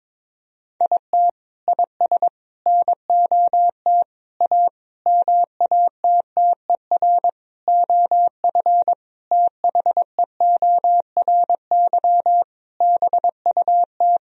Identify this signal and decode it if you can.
Morse code, heard as a keyed tone, about 22 wpm, 700 Hz. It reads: IT IS NOT A MATTER OF THEORY BUT
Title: Sentences from Top 1000 Words - Review Part 4 - 22wpm